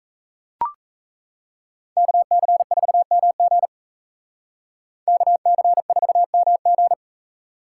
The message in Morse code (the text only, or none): KC4MG KC4MG